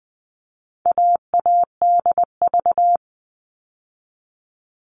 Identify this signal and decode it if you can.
Morse code, heard as a keyed tone, about 20 wpm, 700 Hz. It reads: AADV
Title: All Cut Numbers - 4-Digit 20wpm